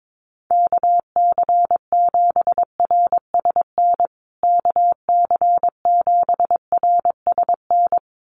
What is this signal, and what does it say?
Morse code, heard as a keyed tone, about 22 wpm, 700 Hz. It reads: KC7RSN KC7RSN
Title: USA Call Signs Repeated 2x 22wpm